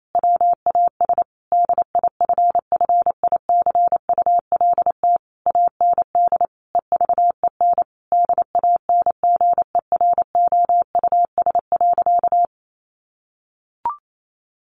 Morse code, heard as a keyed tone, about 28 wpm, 700 Hz. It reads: WAS DIFFICULT AND EVEN DANGEROUS.
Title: Sentences from Top 1200 Words 28wpm